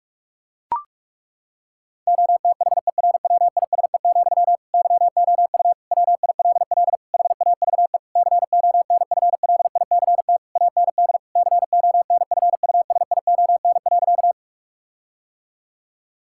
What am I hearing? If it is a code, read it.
Morse code, heard as a keyed tone, about 41 wpm, 700 Hz. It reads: OTHERWISE, YOU WILL HAVE CONFLICT AND CONFUSION.
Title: Sentences from Top 1600 Words 45wpm